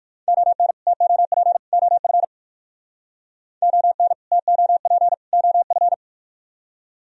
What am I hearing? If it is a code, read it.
Morse code, heard as a keyed tone, about 57 wpm, 700 Hz. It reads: ON TOP OF ON TOP OF